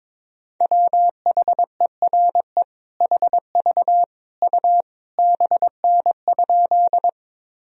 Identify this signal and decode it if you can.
Morse code, heard as a keyed tone, about 22 wpm, 700 Hz. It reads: WHERE HV U BN?
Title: Sentences from Top 200 Words and Abbreviations 22wpm